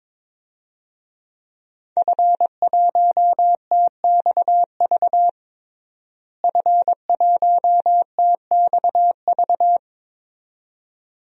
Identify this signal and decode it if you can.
Morse code, heard as a keyed tone, about 22 wpm, 700 Hz. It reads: F1TXV F1TXV
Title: DXCC-100 Call Signs - 5 Characters - Repeated 2x - 22wpm